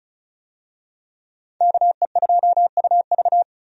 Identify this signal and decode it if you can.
Morse code, heard as a keyed tone, about 36 wpm, 700 Hz. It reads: KE2UV
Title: USA Call Signs - 5 Characters - 35wpm